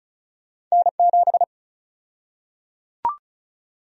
Morse code, heard as a keyed tone, about 34 wpm, 700 Hz. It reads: N7
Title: USA Call Sign Prefixes - Repeated 2x - 35wpm